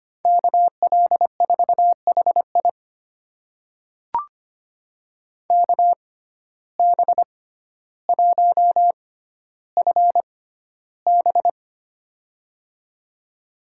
Morse code, heard as a keyed tone, about 25 wpm, spaced as for 13 wpm, 700 Hz. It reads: KL4HI KB1FB
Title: USA Call Signs - 5 Characters - 6x Character Spacing - 25wpm